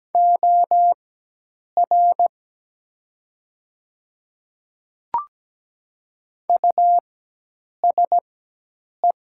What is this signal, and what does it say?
Morse code, heard as a keyed tone, about 17 wpm, spaced as for 8 wpm, 700 Hz. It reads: OR USE
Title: 4x Character Spacing - Top 200 Words - 17wpm